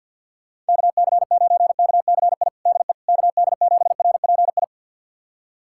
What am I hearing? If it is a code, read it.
Morse code, heard as a keyed tone, about 46 wpm, 700 Hz. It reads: KC9KCI DE KD7RPI
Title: USA Callsign DE Callsign 50wpm